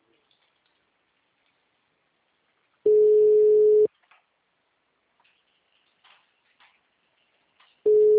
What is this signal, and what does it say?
Ringback tone (Europe, 425 Hz).
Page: Itu